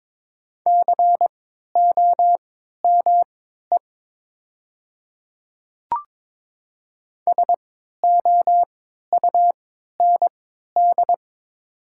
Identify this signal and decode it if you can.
Morse code, heard as a keyed tone, about 22 wpm, spaced as for 12 wpm, 700 Hz. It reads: COME SOUND
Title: 3x Character Spacing - Top 100 Words - 22wpm